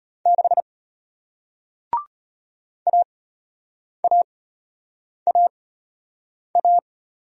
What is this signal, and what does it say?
Morse code, heard as a keyed tone, about 33 wpm, spaced as for 8 wpm, 700 Hz. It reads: 6 AAAA